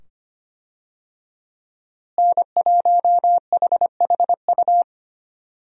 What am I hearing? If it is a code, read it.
Morse code, heard as a keyed tone, about 25 wpm, 700 Hz. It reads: N1HHU
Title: USA Call Signs - 5 Characters - Repeated 2x - 25wpm